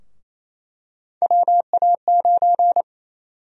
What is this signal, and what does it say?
Morse code, heard as a keyed tone, about 28 wpm, 700 Hz. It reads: WA9
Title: USA Call Sign Prefixes - Repeated 2x - 28wpm